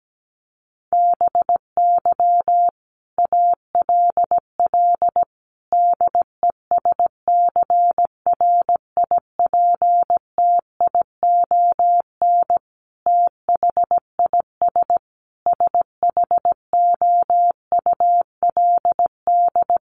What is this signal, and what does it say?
Morse code, heard as a keyed tone, about 17 wpm, 700 Hz. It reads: BY ALL DESCRIPTION THIS SHOULD
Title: Sentences from Top 1300 Words 17wpm